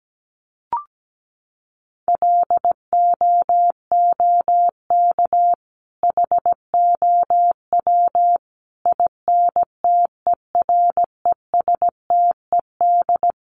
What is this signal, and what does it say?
Morse code, heard as a keyed tone, about 17 wpm, 700 Hz. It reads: LOOK HOW INTERESTED
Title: Sentences from Top 1400 Words 17wpm